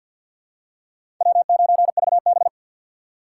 Morse code, heard as a keyed tone, about 51 wpm, 700 Hz. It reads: W9VB
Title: USA Call Signs - 4 Characters - 2x Character Spacing - 50wpm